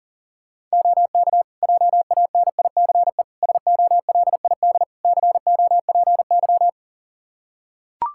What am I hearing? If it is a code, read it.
Morse code, heard as a keyed tone, about 40 wpm, 700 Hz. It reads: OK JANICE SOLID COPY
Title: Practice QSO - Exchange 3 - More Info - 40wpm